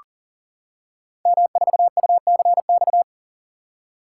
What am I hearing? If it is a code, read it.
Morse code, heard as a keyed tone, about 38 wpm, 700 Hz. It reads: M4UCX
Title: Code Groups - 5 Letter:Numbers - 40wpm